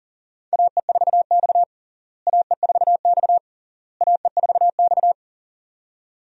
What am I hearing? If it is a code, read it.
Morse code, heard as a keyed tone, about 40 wpm, 700 Hz. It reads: AE4X AE4X AE4X
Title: USA Call Signs - 4 Characters - Repeated 3x - 40wpm